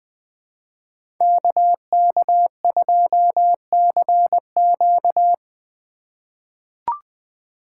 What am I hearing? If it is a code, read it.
Morse code, heard as a keyed tone, about 21 wpm, 700 Hz. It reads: KK2CQ
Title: USA Call Signs - 5 Characters - 8x Character Spacing - 20wpm